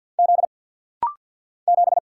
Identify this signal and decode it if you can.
Morse code, heard as a keyed tone, about 48 wpm, 690 Hz. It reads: Z 7
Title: Single Letter-Number - Rapid-Fire 50wpm